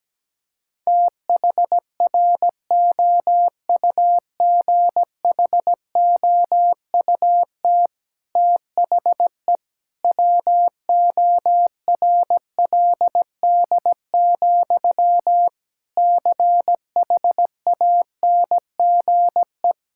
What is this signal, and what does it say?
Morse code, heard as a keyed tone, about 17 wpm, 700 Hz. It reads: THROUGHOUT THE WORLD, CHANGE
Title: Sentences from Top 1000 Words - Review Part 5 - 17wpm